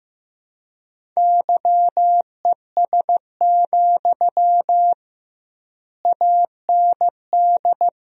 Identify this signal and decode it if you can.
Morse code, heard as a keyed tone, about 16 wpm, 700 Hz. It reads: YES, AND
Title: 2x Word-Spacing - Sentences from Top 500 Words - 15wpm